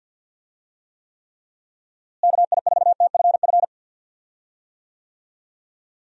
Morse code, heard as a keyed tone, about 51 wpm, 700 Hz. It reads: KI4TFF
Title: USA Call Signs - 6 Characters - Repeated 3x - 50wpm